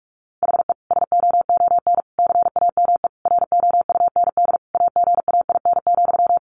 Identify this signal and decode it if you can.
Morse code, heard as a keyed tone, about 44 wpm, 700 Hz. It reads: HE SOON CAME ROUND AGAIN,
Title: Sentences from Top 600 Words 45wpm